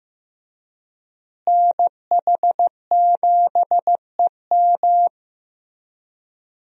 Morse code, heard as a keyed tone, about 15 wpm, 700 Hz. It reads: NH7EM
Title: USA Call Signs - 5 Characters - Repeated 3x - 15wpm